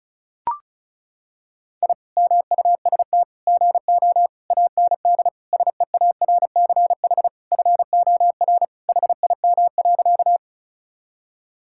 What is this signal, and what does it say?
Morse code, heard as a keyed tone, about 34 wpm, 700 Hz. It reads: I MUST GO AND SEARCH FOR HIM.
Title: Sentences from Top 1000 Words - Review Part 5 - 35wpm